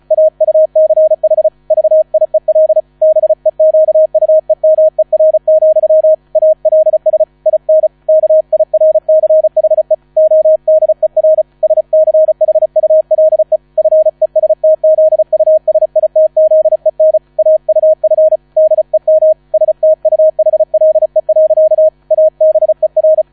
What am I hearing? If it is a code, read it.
Morse code, heard as a keyed tone, about 34 wpm, 630 Hz. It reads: AUCH VIEL BEQUEMER, ALS IN KIRCHE ODER SCHULE FESTZUSITZEN AUF DEM STUHLE. ABER